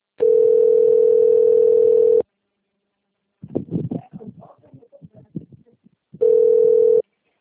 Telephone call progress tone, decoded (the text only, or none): ringback tone